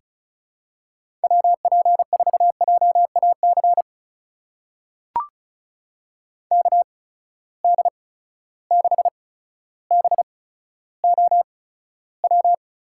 Morse code, heard as a keyed tone, about 36 wpm, spaced as for 15 wpm, 700 Hz. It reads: WP4JAC KD6BOW
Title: USA Call Signs - 6 Characters - 8x Character Spacing - 35wpm